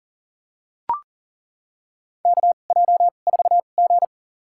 Morse code, heard as a keyed tone, about 40 wpm, spaced as for 29 wpm, 690 Hz. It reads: KJ4G
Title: USA Call Signs - 4 Characters - 2x Character Spacing - 40wpm